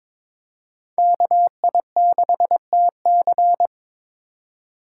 Morse code, heard as a keyed tone, about 22 wpm, 700 Hz. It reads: KI6TC
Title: USA Call Signs - 5 Characters - 2x Character Spacing - 22wpm